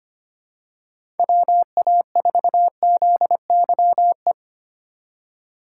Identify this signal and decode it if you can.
Morse code, heard as a keyed tone, about 25 wpm, 700 Hz. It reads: WA4ZYE
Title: USA Call Signs - 6 Characters - 4x Character Spacing - 25wpm